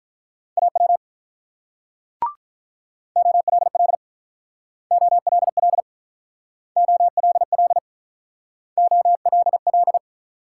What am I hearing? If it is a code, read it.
Morse code, heard as a keyed tone, about 47 wpm, 700 Hz. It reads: AW OLL OLL OLL OLL